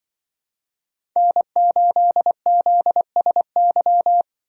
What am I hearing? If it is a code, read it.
Morse code, heard as a keyed tone, about 23 wpm, 700 Hz. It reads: N8ZSY
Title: Speed Racing - USA 1x3 Call Signs 20wpm